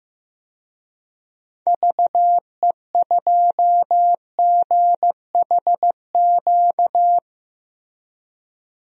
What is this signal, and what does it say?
Morse code, heard as a keyed tone, about 15 wpm, 700 Hz. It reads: VE2GHQ
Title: DXCC-100 Call Signs - 6 Characters - 15wpm